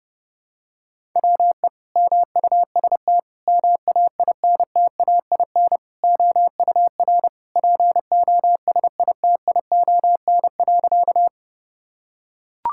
Morse code, heard as a keyed tone, about 30 wpm, 700 Hz. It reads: WE MUST MAINTAIN OUR POSITION.
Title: Sentences from Top 1000 Words - Review Part 5 - 30wpm